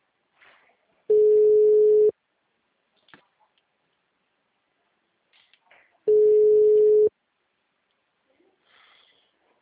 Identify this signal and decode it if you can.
Ringback tone (Europe, 425 Hz).